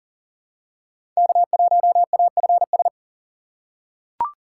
Morse code, heard as a keyed tone, about 40 wpm, 700 Hz. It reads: K1AFS
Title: USA Call Signs - 5 Characters - 2x Character Spacing - 40wpm